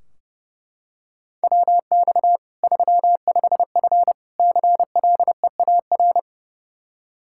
Morse code, heard as a keyed tone, about 30 wpm, 700 Hz. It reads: WX 35F CLEAR
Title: K1USN SST QSO Practice 30wpm